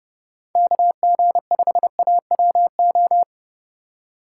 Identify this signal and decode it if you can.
Morse code, heard as a keyed tone, about 31 wpm, 700 Hz. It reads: KG5AWO
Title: USA Call Signs - 6 Characters - 4x Character Spacing - 30wpm